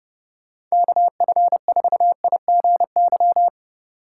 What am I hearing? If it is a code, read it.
Morse code, heard as a keyed tone, about 36 wpm, 700 Hz. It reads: KF4IGY
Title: Speed Racing - USA 2x3 Call Signs 30wpm